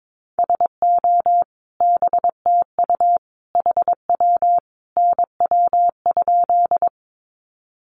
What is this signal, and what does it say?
Morse code, heard as a keyed tone, about 22 wpm, 700 Hz. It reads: SO BTU HW NW?
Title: Common QSO Phrases 22wpm